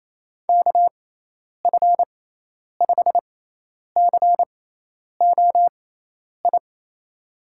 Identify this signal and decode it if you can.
Morse code, heard as a keyed tone, about 28 wpm, spaced as for 10 wpm, 700 Hz. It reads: KF5COI
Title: USA Call Signs - 6 Characters - 6x Character Spacing - 28wpm